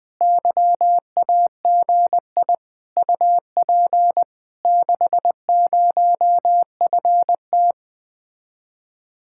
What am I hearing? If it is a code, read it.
Morse code, heard as a keyed tone, about 20 wpm, 700 Hz. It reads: YAGI UP 60FT
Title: Speed-Racing Common QSO Phrases - 20wpm